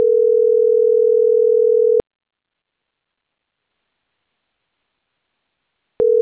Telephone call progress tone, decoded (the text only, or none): ringback tone